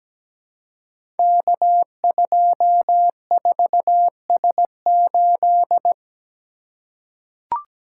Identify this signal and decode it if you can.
Morse code, heard as a keyed tone, about 17 wpm, 700 Hz. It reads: K24S8
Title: Code Groups - 5 Letter:Numbers - 17wpm